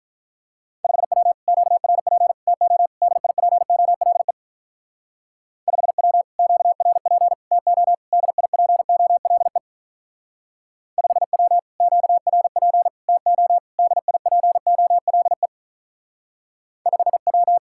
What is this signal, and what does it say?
Morse code, heard as a keyed tone, about 43 wpm, 700 Hz. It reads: 5W QRP TO DIPOLE 5W QRP TO DIPOLE 5W QRP TO DIPOLE 5W
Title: Speed-Racing Common QSO Phrases Encore - 35wpm